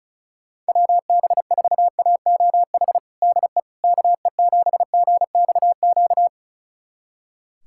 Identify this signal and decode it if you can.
Morse code, heard as a keyed tone, about 34 wpm, 700 Hz. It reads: WB4AOH DE KE7GXQ
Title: USA Callsign DE Callsign 35wpm